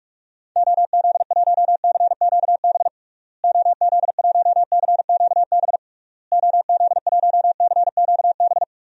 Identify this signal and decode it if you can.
Morse code, heard as a keyed tone, about 46 wpm, 700 Hz. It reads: OZ1CQB OZ1CQB OZ1CQB
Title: DXCC-100 Call Signs - 6 Characters - Repeated 3x - 45wpm